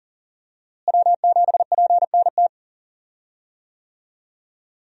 Morse code, heard as a keyed tone, about 38 wpm, 700 Hz. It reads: W7PNT
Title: USA Call Signs - 5 Characters - Repeated 2x - 40wpm